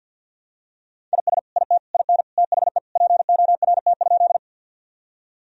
Morse code, heard as a keyed tone, about 48 wpm, 700 Hz. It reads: IS IT IN THE PORT?
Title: Sentences from Top 1000 Words - Review Part 1 - 50wpm